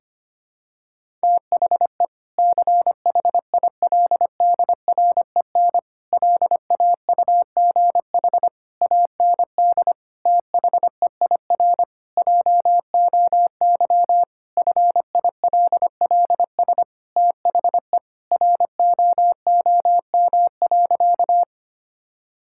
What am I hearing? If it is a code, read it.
Morse code, heard as a keyed tone, about 25 wpm, 700 Hz. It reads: THE CHILDREN LAUGH AND THEIR JOY FILLS THE ROOM.
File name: practice / 25wpm / Sentences with 2-Word Phrases - 25wpm